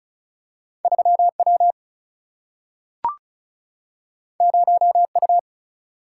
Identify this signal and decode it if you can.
Morse code, heard as a keyed tone, about 36 wpm, 700 Hz. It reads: 3W 0U
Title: Single-Letter Single-Number 35wpm